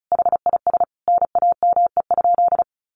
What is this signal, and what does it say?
Morse code, heard as a keyed tone, about 34 wpm, 700 Hz. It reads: HIS NAME?
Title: Sentences from Top 400 Words 35wpm